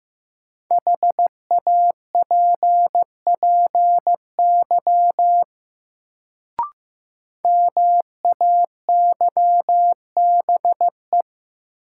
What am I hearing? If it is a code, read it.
Morse code, heard as a keyed tone, about 15 wpm, 700 Hz. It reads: HAPPY MAYBE
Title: Sentences from Top 1000 Words - New Word Drill - 15wpm